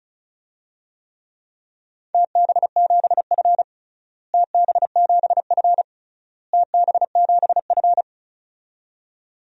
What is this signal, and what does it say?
Morse code, heard as a keyed tone, about 36 wpm, 700 Hz. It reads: TB7F TB7F TB7F
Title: DXCC-100 Call Signs - 4 Characters - Repeated 3x - 35wpm